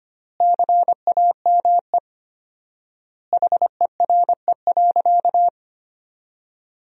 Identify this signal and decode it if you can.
Morse code, heard as a keyed tone, about 25 wpm, 700 Hz. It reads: CAME HERE.
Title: 4x Word-Spacing - Sentences from Top 500 Words - 25wpm